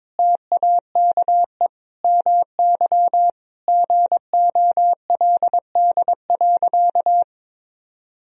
Morse code, heard as a keyed tone, about 22 wpm, 700 Hz. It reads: TAKE MY GOLD.
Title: Sentences from Top 1000 Words - Review Part 3 - 22wpm